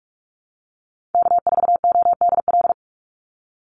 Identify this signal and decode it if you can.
Morse code, heard as a keyed tone, about 44 wpm, 700 Hz. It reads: K4ODL
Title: Call Signs Repeated 3x - 45wpm